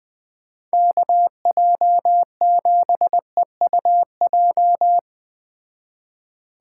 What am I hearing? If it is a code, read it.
Morse code, heard as a keyed tone, about 20 wpm, 700 Hz. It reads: KJ7EUJ
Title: Speed Racing - USA 2x3 Call Signs 17wpm